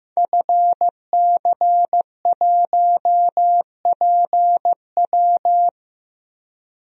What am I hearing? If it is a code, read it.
Morse code, heard as a keyed tone, about 15 wpm, 700 Hz. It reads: FC1PW
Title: DXCC-100 Call Signs - 5 Characters - Repeated 3x - 15wpm